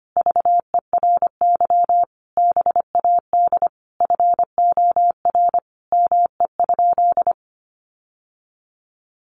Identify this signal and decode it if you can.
Morse code, heard as a keyed tone, about 25 wpm, 700 Hz. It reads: VERY BAD FOR ME?
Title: Sentences from Top 700 Words 25wpm